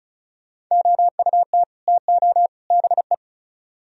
Morse code, heard as a keyed tone, about 44 wpm, 700 Hz. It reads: OUT TO BE